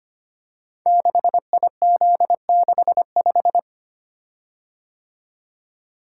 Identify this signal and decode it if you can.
Morse code, heard as a keyed tone, about 25 wpm, 700 Hz. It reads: 6IZ65
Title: Code Groups - 5 Letter:Numbers - 25wpm